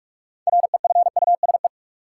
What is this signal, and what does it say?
Morse code, heard as a keyed tone, about 46 wpm, 700 Hz. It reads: REFUSE
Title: Six Letter Words - 45wpm